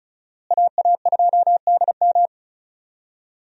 Morse code, heard as a keyed tone, about 34 wpm, 700 Hz. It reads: AA2DM
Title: USA Call Signs - 5 Characters - Repeated 2x - 35wpm